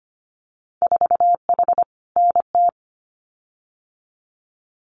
Morse code, heard as a keyed tone, about 25 wpm, 690 Hz. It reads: 4H NT
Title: Winter Field Day Exchanges Encore 25wpm